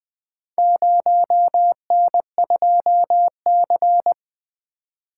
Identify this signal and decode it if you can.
Morse code, heard as a keyed tone, about 20 wpm, 700 Hz. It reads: 0N2C
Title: Four Letter-Number Combinations - 20wpm